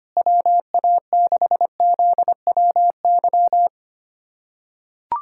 Morse code, heard as a keyed tone, about 25 wpm, 700 Hz. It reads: WA6ZWY
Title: USA Callsign DE Callsign - 6 Characters - 25wpm